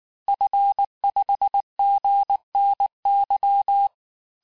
Morse code, heard as a keyed tone, about 19 wpm, 800 Hz. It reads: F5GNY
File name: cours-du-lundi files / cours-1583-fin-QSO-18-mpm